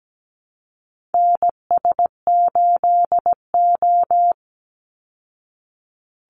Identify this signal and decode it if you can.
Morse code, heard as a keyed tone, about 17 wpm, 700 Hz. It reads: NS8O
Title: Call Signs Repeated 2x - 17wpm